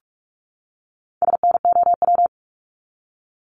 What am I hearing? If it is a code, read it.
Morse code, heard as a keyed tone, about 46 wpm, 700 Hz. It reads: SNOW
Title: Top 500 Words 45wpm